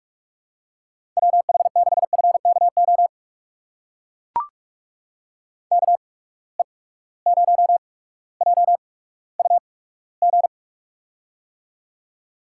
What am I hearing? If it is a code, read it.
Morse code, heard as a keyed tone, about 45 wpm, spaced as for 19 wpm, 700 Hz. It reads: WH6FKO KE0JUG
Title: USA Call Signs - 6 Characters - 8x Character Spacing - 45wpm